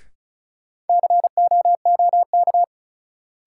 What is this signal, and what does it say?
Morse code, heard as a keyed tone, about 36 wpm, 700 Hz. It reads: COOK
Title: Sentences from Top 700 Words - New Word Drill - 35wpm